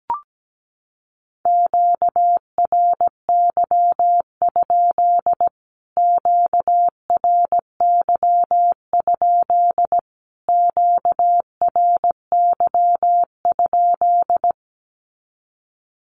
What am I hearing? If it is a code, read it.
Morse code, heard as a keyed tone, about 17 wpm, 700 Hz. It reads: QRY? QRY? QRY?
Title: Q-Codes Repeated 3x 17wpm